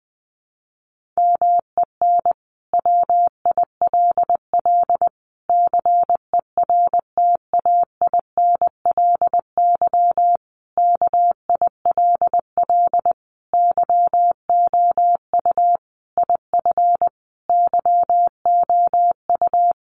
Morse code, heard as a keyed tone, about 20 wpm, 700 Hz. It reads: MEN WILL CERTAINLY KILL YOU IF YOU
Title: Sentences from Top 1500 Words 20wpm